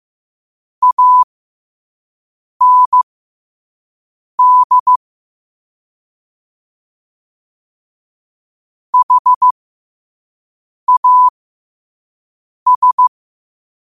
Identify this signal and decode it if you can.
Morse code, heard as a keyed tone, about 15 wpm, spaced as for 5 wpm, 1000 Hz. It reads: AND HAS
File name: qotdfiles / 5wpm